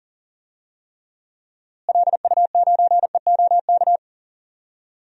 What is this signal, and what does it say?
Morse code, heard as a keyed tone, about 40 wpm, 700 Hz. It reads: LU9EOK
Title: DXCC-100 Call Signs - 6 Characters - Repeated 2x - 40wpm